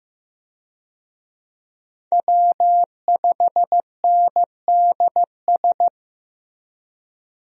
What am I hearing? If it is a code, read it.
Morse code, heard as a keyed tone, about 15 wpm, 700 Hz. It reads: W5NDS
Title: USA Call Signs - 5 Characters - Repeated 2x - 15wpm